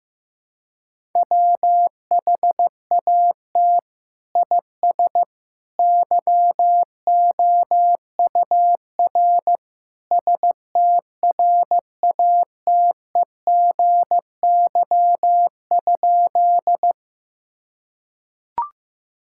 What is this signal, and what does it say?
Morse code, heard as a keyed tone, about 15 wpm, 700 Hz. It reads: WHAT IS YOUR STRATEGY?
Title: Sentences from Top 1000 Words - Review Part 5 - 15wpm